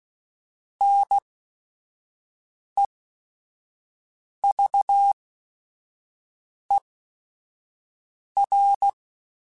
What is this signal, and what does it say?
Morse code, heard as a keyed tone, about 16 wpm, spaced as for 5 wpm, 800 Hz. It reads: NEVER
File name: Quotes / Quotes-mp3s / Advice @5